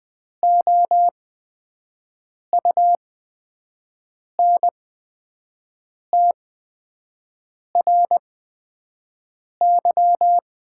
Morse code, heard as a keyed tone, about 20 wpm, spaced as for 5 wpm, 700 Hz. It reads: OUNTRY
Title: 8x Character Spacing - Top 500 Words - 20wpm